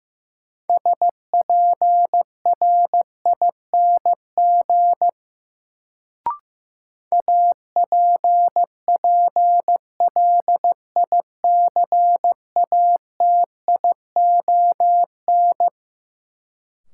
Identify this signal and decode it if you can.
Morse code, heard as a keyed tone, about 15 wpm, 700 Hz. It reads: SPRING APPLICATION
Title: Sentences from Top 1200 Words - New Word Drill - 15wpm